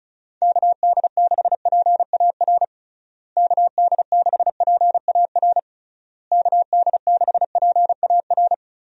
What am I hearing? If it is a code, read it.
Morse code, heard as a keyed tone, about 36 wpm, 700 Hz. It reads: KD6PAR KD6PAR KD6PAR
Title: USA Call Signs - 6 Characters - Repeated 3x - 35wpm